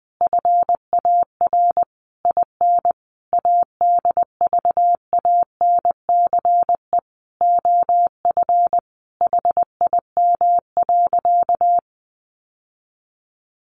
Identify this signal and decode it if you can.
Morse code, heard as a keyed tone, about 20 wpm, 700 Hz. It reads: FAR IN ADVANCE OF HIM.
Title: Sentences from Top 1400 Words 20wpm